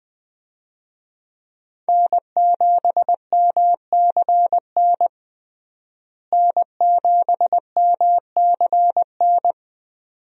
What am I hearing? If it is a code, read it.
Morse code, heard as a keyed tone, about 20 wpm, 700 Hz. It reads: N7MCN N7MCN
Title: USA Call Signs - 5 Characters - Repeated 2x - 20wpm